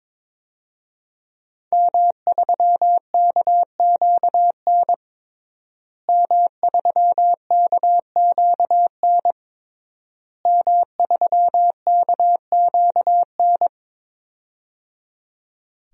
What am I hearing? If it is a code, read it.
Morse code, heard as a keyed tone, about 21 wpm, 700 Hz. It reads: M3KQN M3KQN M3KQN
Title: DXCC-100 Call Signs - 5 Characters - Repeated 3x - 22wpm